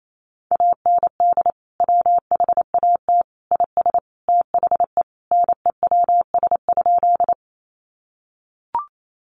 Morse code, heard as a keyed tone, about 28 wpm, 700 Hz. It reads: AND WHAT IS THE NEWS?
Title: Sentences from Top 800 Words 28wpm